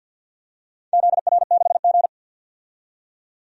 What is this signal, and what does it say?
Morse code, heard as a keyed tone, about 51 wpm, 700 Hz. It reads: ZR6G
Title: DXCC-100 Call Signs - 4 Characters - 50wpm